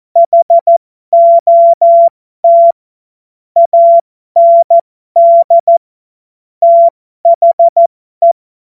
Morse code, heard as a keyed tone, about 14 wpm, spaced as for 12 wpm, 680 Hz. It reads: HOT AND THE